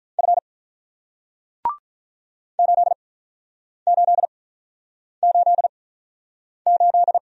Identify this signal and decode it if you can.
Morse code, heard as a keyed tone, about 48 wpm, spaced as for 9 wpm, 700 Hz. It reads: F 8888